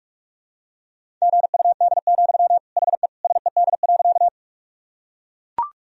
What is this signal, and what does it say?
Morse code, heard as a keyed tone, about 41 wpm, 700 Hz. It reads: GUD, HE SED.
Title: Sentences from Top 200 Words and Abbreviations 45wpm